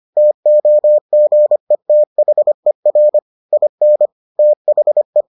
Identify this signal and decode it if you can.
Morse code, heard as a keyed tone, about 25 wpm, 600 Hz. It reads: TOGETHER IN THE